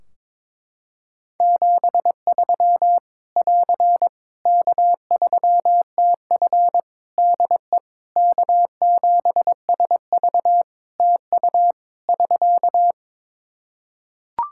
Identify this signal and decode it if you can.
Morse code, heard as a keyed tone, about 22 wpm, 700 Hz. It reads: 73 <AR> K3TF DE K7SV TU <SK>
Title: Practice QSO - Exchange 4 - Ending - 22wpm